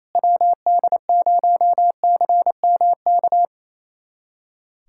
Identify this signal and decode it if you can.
Morse code, heard as a keyed tone, about 27 wpm, 700 Hz. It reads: WD0CMK
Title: USA Callsign DE Callsign - 6 Characters - 28wpm